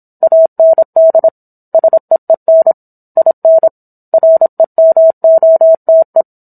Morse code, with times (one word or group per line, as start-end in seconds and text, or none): 0.2-1.3 AND
1.7-2.7 SEEN
3.2-3.7 IN
4.1-6.2 REMOTE